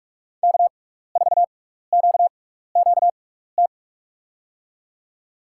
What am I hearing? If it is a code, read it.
Morse code, heard as a keyed tone, about 46 wpm, spaced as for 15 wpm, 700 Hz. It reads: K4QQT
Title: USA Call Signs - 5 Characters - 6x Character Spacing - 45wpm